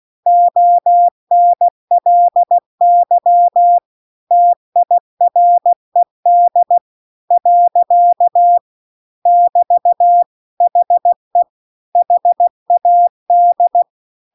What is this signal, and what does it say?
Morse code, heard as a keyed tone, about 16 wpm, 700 Hz. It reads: ONLY TIRED . <BT> HE HAD
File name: CW-The-Big-Twoheaded-River-2-16WPM